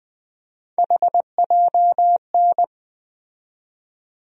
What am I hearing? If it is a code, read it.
Morse code, heard as a keyed tone, about 21 wpm, 700 Hz. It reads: HJN
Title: USA Call Sign Postfixes - 3 Letters 20wpm